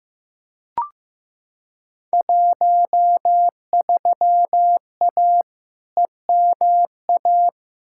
Morse code, heard as a keyed tone, about 15 wpm, 700 Hz. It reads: 13A EMA
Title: Field Day Exchanges Encore - 15wpm